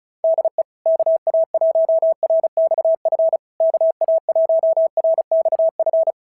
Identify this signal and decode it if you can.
Morse code, heard as a keyed tone, about 33 wpm, 660 Hz. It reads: DE KA1RXF KA1RXF
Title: SOTA QSO Practice 35wpm